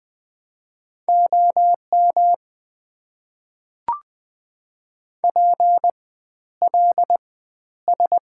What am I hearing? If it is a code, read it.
Morse code, heard as a keyed tone, about 20 wpm, spaced as for 11 wpm, 690 Hz. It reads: OM PLS
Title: Top 90 QSO Elements - 4x Character Spacing - 20wpm